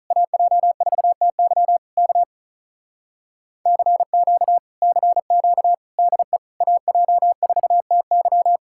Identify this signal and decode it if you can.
Morse code, heard as a keyed tone, about 48 wpm, 700 Hz. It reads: AJ4TY K CQ CQ DE AJ4TY